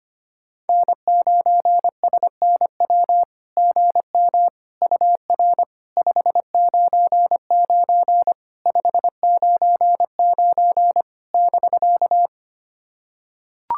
Morse code, heard as a keyed tone, about 25 wpm, 700 Hz. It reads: N9SNW GM UR 599 599 <BK>
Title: SOTA QSO Practice 25wpm